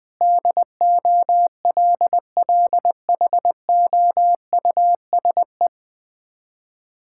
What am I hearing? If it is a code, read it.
Morse code, heard as a keyed tone, about 20 wpm, 700 Hz. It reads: DOLLHOUSE
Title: Difficult Words with Hints - 20wpm